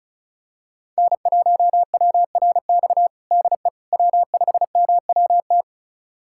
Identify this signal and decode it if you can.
Morse code, heard as a keyed tone, about 34 wpm, 700 Hz. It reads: N1WRX DE W5MWT
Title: USA Callsign DE Callsign - 5 Characters - 35wpm